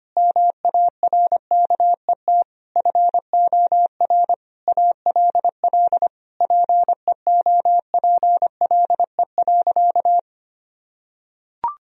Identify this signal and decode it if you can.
Morse code, heard as a keyed tone, about 25 wpm, 700 Hz. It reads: MARKET FOR ALL PEOPLE.
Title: Sentences from Top 1000 Words - Review Part 3 - 25wpm